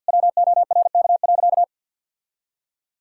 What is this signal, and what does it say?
Morse code, heard as a keyed tone, about 50 wpm, 700 Hz. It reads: WORK.
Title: Sentences from Top 1000 Words - Review Part 6 - 50wpm